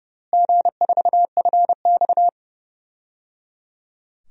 Morse code, heard as a keyed tone, about 30 wpm, 700 Hz. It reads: G4FX
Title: DXCC-100 Call Signs - 4 Characters - Repeated 3x - 30wpm